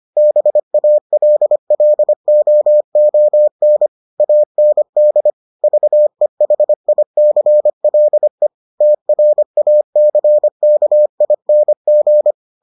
Morse code, heard as a keyed tone, about 25 wpm, 600 Hz. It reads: BALLOON AND VEHICLE TRACKING